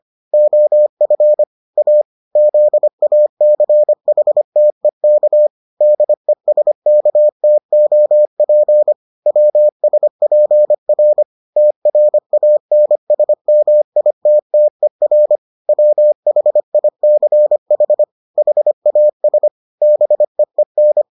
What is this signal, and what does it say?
Morse code, heard as a keyed tone, about 25 wpm, 600 Hz. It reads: OF A ZACHTEK DESKTOP WSPR TRANSMITTER WHICH HAS BEEN